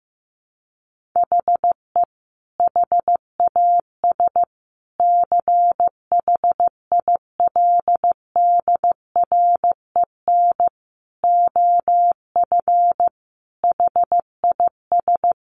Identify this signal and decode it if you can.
Morse code, heard as a keyed tone, about 15 wpm, 700 Hz. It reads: HE HAS CHILDREN OF HIS
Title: Sentences from Top 300 Words 15wpm